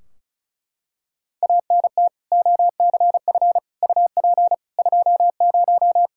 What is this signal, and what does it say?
Morse code, heard as a keyed tone, about 36 wpm, 700 Hz. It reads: ANT OCF UP 20
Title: Practice QSO - Exchange 2 - Info - 35wpm